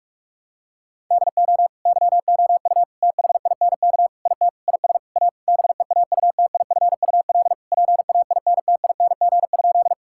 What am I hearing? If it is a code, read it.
Morse code, heard as a keyed tone, about 46 wpm, 690 Hz. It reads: DO YOU THINK IT IS A BEAUTIFUL PAINTING?
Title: Sentences from Top 1000 Words - Review Part 6 - 45wpm